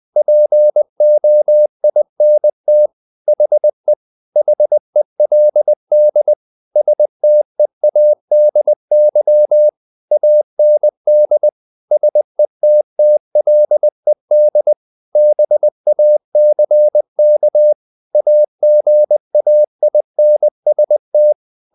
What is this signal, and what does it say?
Morse code, heard as a keyed tone, about 20 wpm, 600 Hz. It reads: POINT HE HELD STEADY AND SETTLED BACK AGAINST
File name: the old man and the sea / the old man and the sea part 1 20wpm